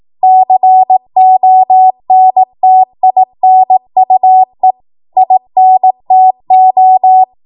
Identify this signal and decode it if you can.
Morse code, heard as a keyed tone, about 18 wpm, 750 Hz. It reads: CONTINUE INTO
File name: Morse / 18WPM